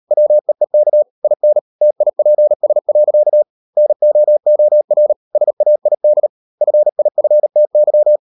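Morse code, heard as a keyed tone, about 38 wpm, 600 Hz. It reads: WEEK IN TIPS. NOOR SAID FIFTY
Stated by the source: Morse code practice at 38 WPM, sending Ken Follett's novel Never in English, from CW med Gnister.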